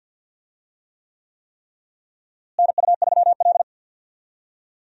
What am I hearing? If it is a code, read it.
Morse code, heard as a keyed tone, about 47 wpm, 700 Hz. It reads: NU3L
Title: USA Call Signs - 4 Characters - Repeated 3x - 50wpm